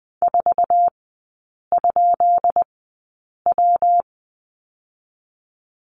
Morse code, heard as a keyed tone, about 20 wpm, spaced as for 8 wpm, 690 Hz. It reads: 4?W
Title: 133 Lesson 22e - Question Mark - Three Characters 20wpm